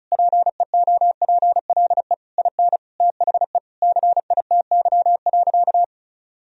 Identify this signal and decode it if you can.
Morse code, heard as a keyed tone, about 36 wpm, 700 Hz. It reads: PEOPLE IN THE CITY.
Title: Sentences from Top 1000 Words - Review Part 1 - 35wpm